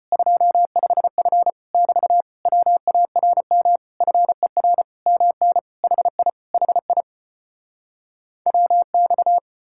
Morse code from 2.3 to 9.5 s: WARM FER MN HI HI WX

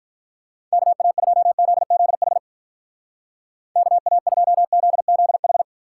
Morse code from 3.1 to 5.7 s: KA2ZZH

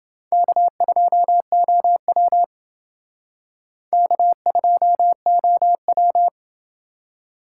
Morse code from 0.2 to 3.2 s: K2OW

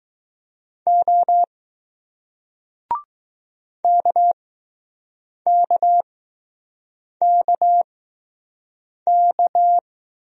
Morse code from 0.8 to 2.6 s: O

